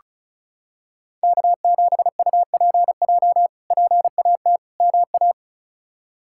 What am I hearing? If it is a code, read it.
Morse code, heard as a keyed tone, about 36 wpm, 700 Hz. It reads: K7UPJ PAT MA
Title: K1USN SST QSO Practice - 35wpm